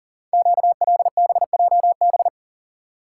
Morse code, heard as a keyed tone, about 42 wpm, 700 Hz. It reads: QLBJB
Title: Code Groups - 5 Letters - 40wpm